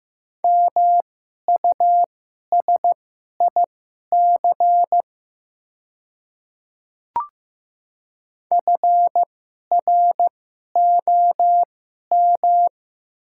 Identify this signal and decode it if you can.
Morse code, heard as a keyed tone, about 15 wpm, spaced as for 11 wpm, 700 Hz. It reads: MUSIC FROM
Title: 2x Character Spacing - Top 300 Words - 15wpm